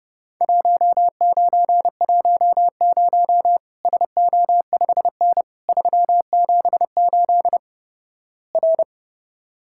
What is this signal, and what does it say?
Morse code, heard as a keyed tone, about 32 wpm, 700 Hz. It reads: 1910 SO5N 378 R
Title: Worked All Europe DX Contest - QTC Exchange - 30wpm